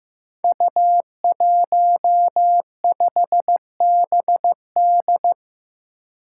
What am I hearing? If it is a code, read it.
Morse code, heard as a keyed tone, about 15 wpm, 700 Hz. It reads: U15BD
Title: Five Challenging Letter-Number Combinations 15wpm